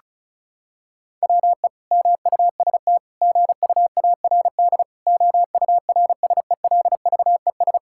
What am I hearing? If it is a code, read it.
Morse code, heard as a keyed tone, about 36 wpm, 700 Hz. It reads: WE MUST GUARD OURSELVES
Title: Sentences from Top 1600 Words 35wpm